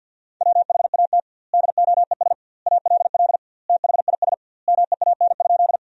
Morse code, heard as a keyed tone, about 49 wpm, 700 Hz. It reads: WHAT DOES ALL THIS MEAN?
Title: 2x Word-Spacing - Sentences from Top 200 Words - 50wpm